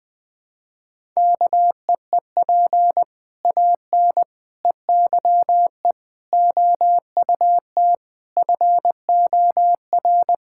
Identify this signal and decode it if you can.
Morse code, heard as a keyed tone, about 19 wpm, 700 Hz. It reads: KEEP AN EYE OUT FOR
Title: Sentences with 6-Word Phrases - 20wpm